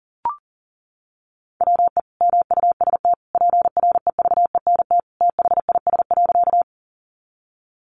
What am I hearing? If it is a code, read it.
Morse code, heard as a keyed tone, about 40 wpm, 700 Hz. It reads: WE MUST PREVENT THIS.
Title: Sentences from Top 1100 Words 40wpm